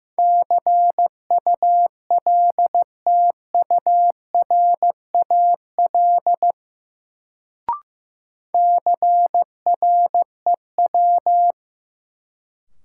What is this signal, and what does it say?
Morse code, heard as a keyed tone, about 15 wpm, 700 Hz. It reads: CULTURAL CREW
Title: Sentences from Top 1500 Words - New Word Drill - 15wpm